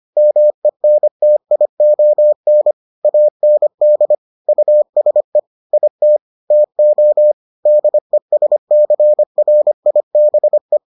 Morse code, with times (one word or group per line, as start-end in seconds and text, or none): 0.2-2.7 MENTION
3.0-4.1 AND
4.5-5.4 USE
5.7-6.2 IT
6.5-7.3 TO
7.7-10.8 DESCRIBE